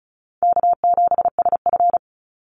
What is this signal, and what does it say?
Morse code, heard as a keyed tone, about 34 wpm, 700 Hz. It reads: K7SF
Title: Easy Call Signs Repeated 3x - 35wpm